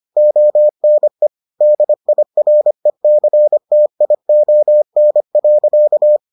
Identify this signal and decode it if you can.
Morse code, heard as a keyed tone, about 25 wpm, 600 Hz. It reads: ONE DIRECTION.